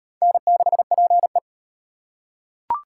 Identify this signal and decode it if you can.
Morse code, heard as a keyed tone, about 38 wpm, 700 Hz. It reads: N6PE